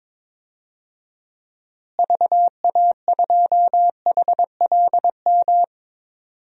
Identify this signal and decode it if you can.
Morse code, heard as a keyed tone, about 22 wpm, 700 Hz. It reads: VA2HLM
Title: DXCC-100 Call Signs - 6 Characters - Repeated 3x - 22wpm